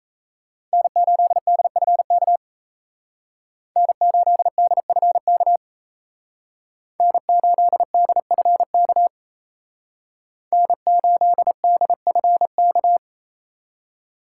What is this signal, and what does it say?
Morse code, heard as a keyed tone, about 37 wpm, 700 Hz. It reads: N8DFK N8DFK N8DFK N8DFK